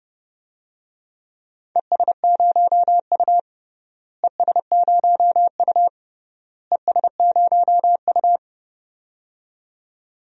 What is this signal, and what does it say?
Morse code, heard as a keyed tone, about 32 wpm, 700 Hz. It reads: ES0U ES0U ES0U
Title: DXCC-100 Call Signs - 4 Characters - Repeated 3x - 30wpm